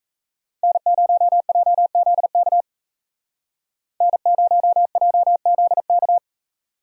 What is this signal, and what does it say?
Morse code, heard as a keyed tone, about 37 wpm, 700 Hz. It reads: N0JZK N0JZK